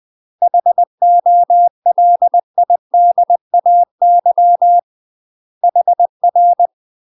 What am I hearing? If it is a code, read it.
Morse code, heard as a keyed tone, about 20 wpm, 700 Hz. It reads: HOLIDAY HR
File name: UK MASTER QSOs TRAINING / 20WPM / UK MASTER QSOs 0010